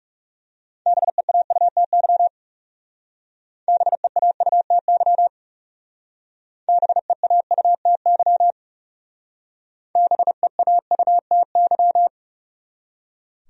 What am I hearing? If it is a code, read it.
Morse code, heard as a keyed tone, about 38 wpm, 700 Hz. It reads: BEAUTY BEAUTY BEAUTY BEAUTY
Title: Speed-Racing Top 500 Words 30wpm